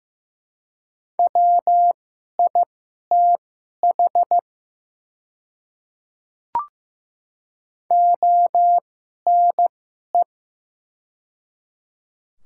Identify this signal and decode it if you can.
Morse code, heard as a keyed tone, about 16 wpm, spaced as for 11 wpm, 700 Hz. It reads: WITH ONE
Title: 2x Character Spacing - Top 300 Words - 15wpm